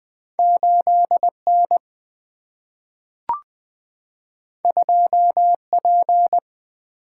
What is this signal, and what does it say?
Morse code, heard as a keyed tone, about 20 wpm, 700 Hz. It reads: 8 N 2 P
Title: Single-Letter Single-Number 20wpm